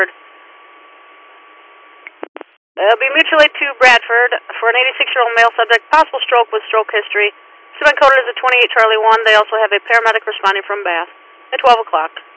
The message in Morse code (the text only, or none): none